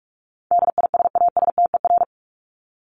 Morse code, heard as a keyed tone, about 44 wpm, 700 Hz. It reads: DISASTER
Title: Top 1501-1750 Words 45wpm